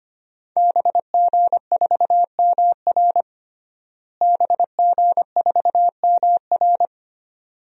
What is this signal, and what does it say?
Morse code, heard as a keyed tone, about 25 wpm, 700 Hz. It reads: BG4MR BG4MR
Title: DXCC-100 Call Signs - 5 Characters - Repeated 2x - 25wpm